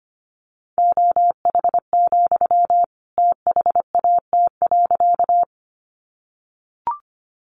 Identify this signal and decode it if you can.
Morse code, heard as a keyed tone, about 25 wpm, 700 Hz. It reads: OH, THAT.
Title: Sentences from Top 500 Words 25wpm